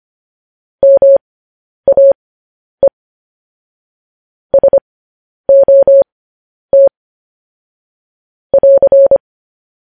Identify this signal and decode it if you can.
Morse code, heard as a keyed tone, about 25 wpm, spaced as for 10 wpm, 550 Hz. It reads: MAE SOT <AR>
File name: podcast / News headlines-1x-25x10